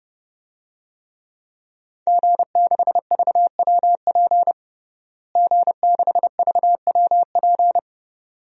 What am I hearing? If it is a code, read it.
Morse code, heard as a keyed tone, about 30 wpm, 700 Hz. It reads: G6VWP G6VWP
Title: DXCC-100 Call Signs - 5 Characters - Repeated 2x - 30wpm